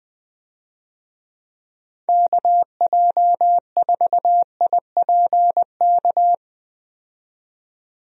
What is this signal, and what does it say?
Morse code, heard as a keyed tone, about 20 wpm, 700 Hz. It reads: KJ4IPK
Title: USA Call Signs - 6 Characters - 20wpm